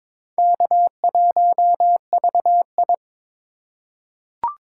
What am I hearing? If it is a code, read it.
Morse code, heard as a keyed tone, about 21 wpm, 700 Hz. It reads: K1VI
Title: USA Call Signs - 4 Characters - 4x Character Spacing - 22wpm